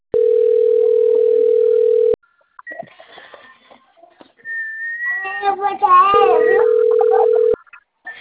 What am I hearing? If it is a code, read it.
Ringback tone (North America), 440 + 480 Hz.